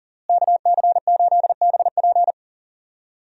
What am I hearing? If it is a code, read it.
Morse code, heard as a keyed tone, about 37 wpm, 700 Hz. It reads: KC8BP
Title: USA Call Signs - 5 Characters - Repeated 2x - 40wpm